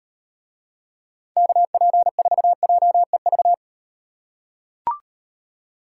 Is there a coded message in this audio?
Yes: Morse code from 1.4 to 3.5 s, heard as a keyed tone, about 37 wpm, 700 Hz. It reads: KP4JEV